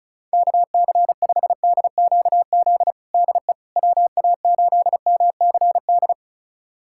Morse code, heard as a keyed tone, about 36 wpm, 700 Hz. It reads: KC5DQZ DE WA8MCD
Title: USA Callsign DE Callsign 35wpm